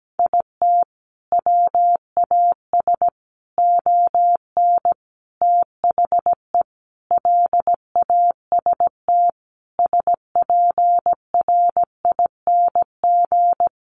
Morse code, heard as a keyed tone, about 17 wpm, 700 Hz. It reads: IT WAS ON THE LAST SPRING
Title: Sentences from Top 1200 Words 17wpm